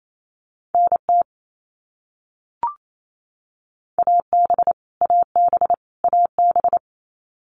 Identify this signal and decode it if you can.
Morse code, heard as a keyed tone, about 28 wpm, 700 Hz. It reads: NT AB AB AB
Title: Canadian Provinces and Territories - Repeated 3x - 28wpm